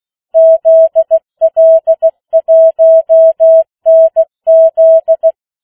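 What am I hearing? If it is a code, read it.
Morse code, heard as a keyed tone, about 16 wpm, 650 Hz. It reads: ZL1NZ
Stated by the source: Opening of the New Zealand national amateur radio net at 15 wpm, which meets on 3535.0 kHz.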